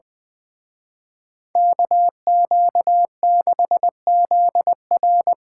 Morse code, heard as a keyed tone, about 24 wpm, 700 Hz. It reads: KQ6ZR
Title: Speed Racing - USA Call Signs 20wpm